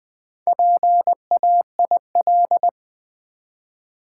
Morse code, heard as a keyed tone, about 20 wpm, 700 Hz. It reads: PAIL
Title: Easy Words with Hints - 20wpm